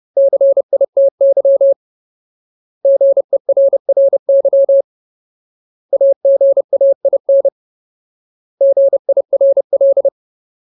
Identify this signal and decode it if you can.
Morse code, heard as a keyed tone, about 30 wpm, 550 Hz. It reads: CITY GERRY AGAIN GIRL
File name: Ultimate Word List / 30WPM Ultimate